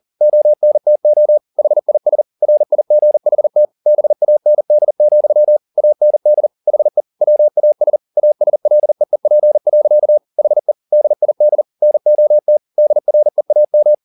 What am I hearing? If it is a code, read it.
Morse code, heard as a keyed tone, about 40 wpm, 600 Hz. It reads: ONTO HIS RIGHT BAND, AND HE WAS ASLEEP. HE DID NOT DREAM
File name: the old man and the sea 2 40wpm